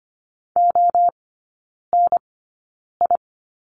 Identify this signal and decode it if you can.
Morse code, heard as a keyed tone, about 25 wpm, spaced as for 9 wpm, 700 Hz. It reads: ONI
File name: course / 25:10wpm / 100 Lesson 17e - Letter u - Three Characters 25wpm